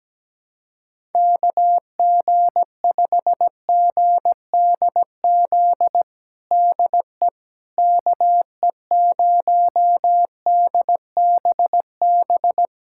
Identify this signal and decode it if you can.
Morse code, heard as a keyed tone, about 17 wpm, 700 Hz. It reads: KG5GDZ DE KE0DBB
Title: USA Callsign DE Callsign - 6 Characters - 17wpm